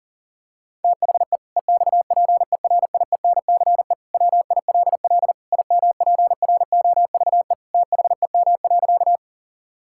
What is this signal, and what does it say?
Morse code, heard as a keyed tone, about 40 wpm, 700 Hz. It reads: THE EXPERIENCE WILL IMPROVE THEM.
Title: Sentences from Top 1000 Words - Review Part 5 - 40wpm